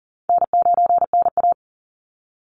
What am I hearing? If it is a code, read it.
Morse code, heard as a keyed tone, about 40 wpm, 700 Hz. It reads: N9NA
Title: Easy Call Signs - 40wpm